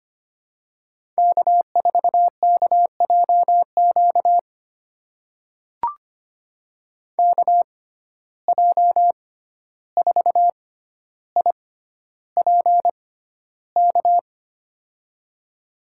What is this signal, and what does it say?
Morse code, heard as a keyed tone, about 25 wpm, spaced as for 12 wpm, 700 Hz. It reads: K4KJQ KJ4IPK
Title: USA Call Signs - 6x Character Spacing - 25wpm